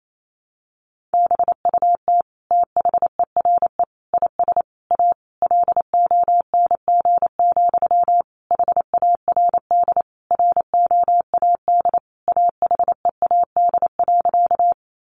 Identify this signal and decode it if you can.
Morse code, heard as a keyed tone, about 28 wpm, 700 Hz. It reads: BUT THERE IS A LONG, HARD ROAD AHEAD.
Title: Sentences from Top 1100 Words 28wpm